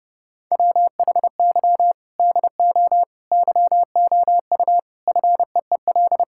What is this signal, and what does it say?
Morse code, heard as a keyed tone, about 30 wpm, 700 Hz. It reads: WHY DO YOU FEEL
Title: Sentences from Top 1600 Words 30wpm